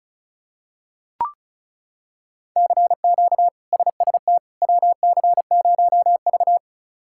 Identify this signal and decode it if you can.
Morse code, heard as a keyed tone, about 36 wpm, 700 Hz. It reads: CQ SST WC0V
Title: K1USN SST QSO Practice - 35wpm